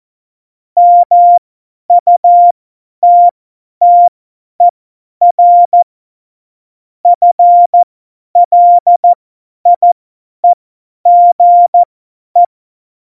Morse code, heard as a keyed tone, about 14 wpm, spaced as for 10 wpm, 700 Hz. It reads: MUTTER FLIEGE
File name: Morsekurs / Witze / Witz 10.Juli 2018, 50 F70